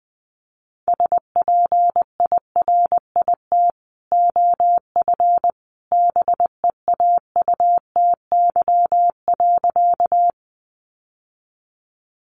Morse code, heard as a keyed tone, about 20 wpm, 700 Hz. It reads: SPIRIT OF BEAUTY.
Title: Sentences from Top 1400 Words 20wpm